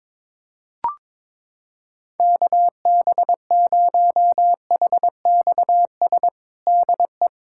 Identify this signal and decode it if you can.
Morse code, heard as a keyed tone, about 22 wpm, 700 Hz. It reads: KB0HXS DE
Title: USA Callsign DE Callsign - 6 Characters - 22wpm